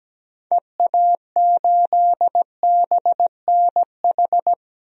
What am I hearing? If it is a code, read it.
Morse code, heard as a keyed tone, about 17 wpm, 700 Hz. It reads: EA8BNH
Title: DXCC-100 Call Signs - 6 Characters - Repeated 3x - 17wpm